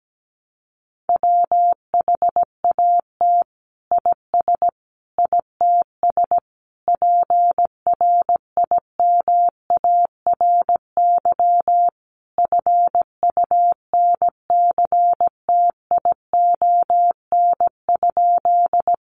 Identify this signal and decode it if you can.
Morse code, heard as a keyed tone, about 17 wpm, 700 Hz. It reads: WHAT IS ITS PRIMARY FUNCTION?
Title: Sentences from Top 1100 Words 17wpm